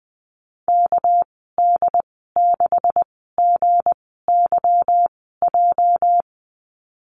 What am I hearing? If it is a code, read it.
Morse code, heard as a keyed tone, about 20 wpm, spaced as for 15 wpm, 700 Hz. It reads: KD6GYJ
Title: 219 Lesson 35g - Number 0 - Call Signs 20wpm